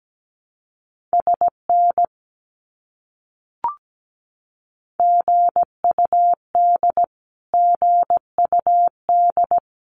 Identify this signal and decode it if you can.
Morse code, heard as a keyed tone, about 17 wpm, 700 Hz. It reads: SN GUD GUD
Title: Abbreviations Repeated 3x - 17wpm